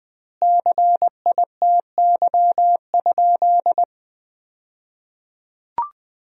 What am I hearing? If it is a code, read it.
Morse code, heard as a keyed tone, about 20 wpm, 700 Hz. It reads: CITY?
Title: 2x Word-Spacing - Sentences from Top 300 Words - 20wpm